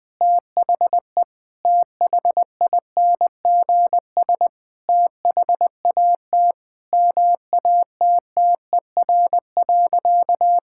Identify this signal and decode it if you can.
Morse code, heard as a keyed tone, about 21 wpm, 700 Hz. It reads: THE THINGS THAT MATTER.
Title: Sentences with 4-Word Phrases - Spoken Beginning - 20wpm